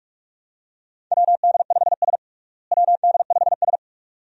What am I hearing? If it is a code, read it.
Morse code, heard as a keyed tone, about 43 wpm, 700 Hz. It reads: WD5S WD5S
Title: USA Call Signs - 4 Characters - Repeated 2x - 45wpm